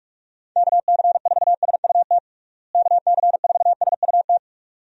Morse code, heard as a keyed tone, about 46 wpm, 700 Hz. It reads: KC4SUT KC4SUT
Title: USA Call Signs - 6 Characters - Repeated 2x - 45wpm